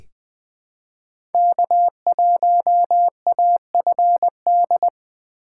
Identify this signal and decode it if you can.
Morse code, heard as a keyed tone, about 20 wpm, 700 Hz. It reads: K1AFD
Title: USA Call Signs - 3x Character Spacing - 20wpm